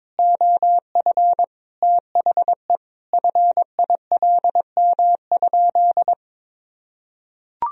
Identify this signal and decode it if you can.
Morse code, heard as a keyed tone, about 21 wpm, 700 Hz. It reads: OF THE FILM?
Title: Sentences from Top 1000 Words - Review Part 3 - 22wpm